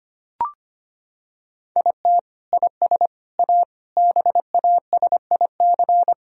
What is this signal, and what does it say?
Morse code, heard as a keyed tone, about 25 wpm, 700 Hz. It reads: IT IS A BASIC
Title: Sentences from Top 1000 Words - Review Part 3 - 25wpm